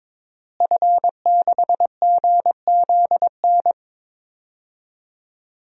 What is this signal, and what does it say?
Morse code, heard as a keyed tone, about 22 wpm, 700 Hz. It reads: F6GZN
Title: DXCC-100 Call Signs - 5 Characters - Repeated 3x - 22wpm